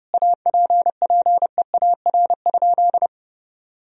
Morse code, heard as a keyed tone, about 30 wpm, 700 Hz. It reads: APPEAR?
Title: Sentences from Top 1000 Words - Review Part 2 - 30wpm